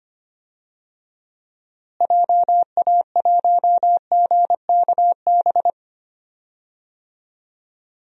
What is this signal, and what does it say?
Morse code, heard as a keyed tone, about 25 wpm, 700 Hz. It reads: JA1GKB
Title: DXCC-100 Call Signs - 6 Characters - Repeated 2x - 25wpm